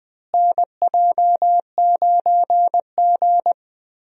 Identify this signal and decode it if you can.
Morse code, heard as a keyed tone, about 20 wpm, 700 Hz. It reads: NJ9G
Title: USA Call Signs - 4 Characters - Repeated 2x - 20wpm